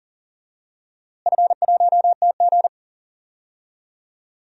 Morse code, heard as a keyed tone, about 40 wpm, 700 Hz. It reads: F1TG
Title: DXCC-100 Call Signs - 4 Characters - Repeated 2x - 40wpm